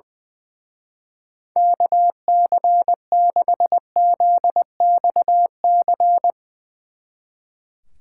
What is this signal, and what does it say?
Morse code, heard as a keyed tone, about 21 wpm, 700 Hz. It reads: KC6ZXC
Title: Speed Racing - USA 2x3 Call Signs 20wpm